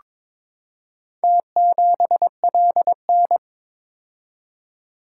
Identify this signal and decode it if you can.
Morse code, heard as a keyed tone, about 22 wpm, 700 Hz. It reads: T7LN
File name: practice / 22wpm / Four Letter-Number Combinations - 22wpm